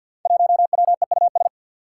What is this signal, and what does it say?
Morse code, heard as a keyed tone, about 49 wpm, 700 Hz. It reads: 1WEUS
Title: Five Challenging Letter-Number Combinations 50wpm